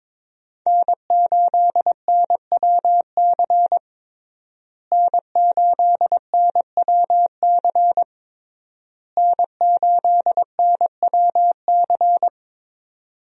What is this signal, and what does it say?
Morse code, heard as a keyed tone, about 22 wpm, 700 Hz. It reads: N8NWC N8NWC N8NWC
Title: USA Call Signs - 5 Characters - Repeated 3x - 22wpm